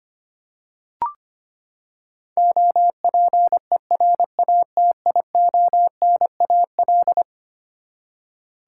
Morse code, heard as a keyed tone, about 25 wpm, 700 Hz. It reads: OPERATIONAL
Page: morse-code-ninja-practice-25wpm